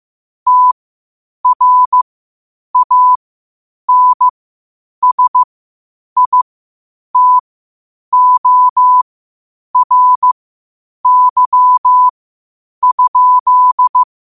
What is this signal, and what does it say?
Morse code, heard as a keyed tone, about 15 wpm, spaced as for 8 wpm, 1000 Hz. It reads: TRANSITORY?
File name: qotdfiles / 8wpm